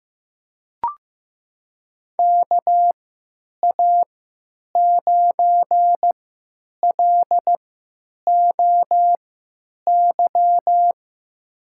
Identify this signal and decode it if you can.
Morse code, heard as a keyed tone, about 15 wpm, spaced as for 8 wpm, 700 Hz. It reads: KA9LOY
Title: USA Call Signs - 6 Characters - 3x Character Spacing - 15wpm